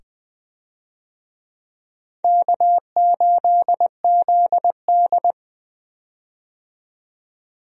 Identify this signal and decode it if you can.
Morse code, heard as a keyed tone, about 20 wpm, 700 Hz. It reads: K8ZD
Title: USA Call Signs - 4 Characters - 20wpm